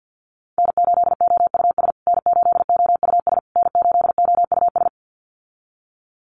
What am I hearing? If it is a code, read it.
Morse code, heard as a keyed tone, about 50 wpm, 700 Hz. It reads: N8OUS N8OUS N8OUS
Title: USA Call Signs Repeated 3x 50wpm